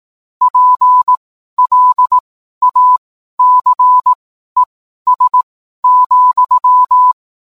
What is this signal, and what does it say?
Morse code, heard as a keyed tone, about 18 wpm, spaced as for 12 wpm, 1000 Hz. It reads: PLACES,
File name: cwblog / cw mp3 directory / mem morse 12